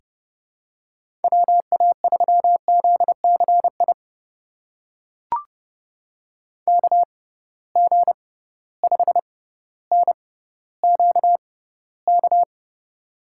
Morse code, heard as a keyed tone, about 31 wpm, spaced as for 16 wpm, 700 Hz. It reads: WA3ZCI KG5NQK
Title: USA Call Signs - 6 Characters - 6x Character Spacing - 30wpm